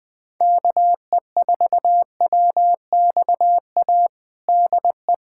Morse code, heard as a keyed tone, about 21 wpm, 700 Hz. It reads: KE4WXA DE
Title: USA Callsign DE Callsign - 6 Characters - 20wpm